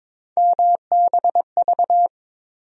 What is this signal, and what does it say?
Morse code, heard as a keyed tone, about 22 wpm, 700 Hz. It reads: MBV
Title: USA Call Sign Postfixes - 3 Letters 22wpm